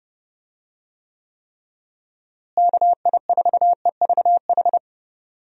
Morse code, heard as a keyed tone, about 30 wpm, 700 Hz. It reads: KI4EVH
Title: USA Call Signs - 6 Characters - Repeated 3x - 30wpm